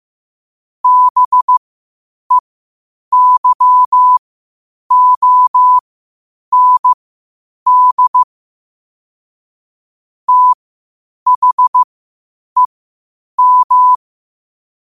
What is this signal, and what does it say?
Morse code, heard as a keyed tone, about 15 wpm, spaced as for 8 wpm, 1000 Hz. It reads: BEYOND THEM
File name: qotdfiles / 8wpm